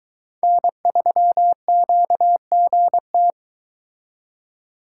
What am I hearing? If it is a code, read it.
Morse code, heard as a keyed tone, about 22 wpm, 700 Hz. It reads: N3QGT